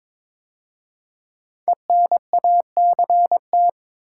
Morse code, heard as a keyed tone, about 22 wpm, 700 Hz. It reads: ENACT
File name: 5 Letter Words - First 3 Letters Given - 22wpm